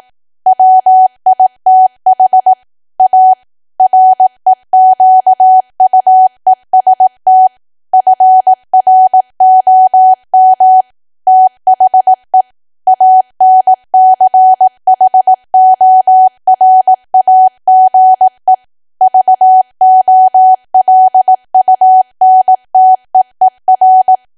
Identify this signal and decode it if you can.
Morse code, heard as a keyed tone, about 18 wpm, 750 Hz. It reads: WITH A REQUEST FROM THE ANCHORAGE VOLUNTEER
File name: cwa / W1AW-Level2 / 18WPM